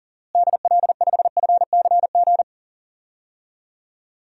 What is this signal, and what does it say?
Morse code, heard as a keyed tone, about 40 wpm, 700 Hz. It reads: DL5FCG
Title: DXCC-100 Call Signs - 6 Characters - Repeated 2x - 40wpm